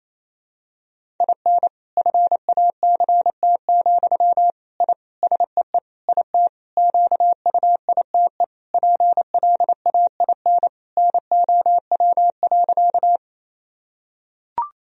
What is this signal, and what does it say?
Morse code, heard as a keyed tone, about 27 wpm, 700 Hz. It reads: IN FACT, I SEE IT QUITE PLAIN NOW.
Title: Sentences from Top 1000 Words - Review Part 3 - 28wpm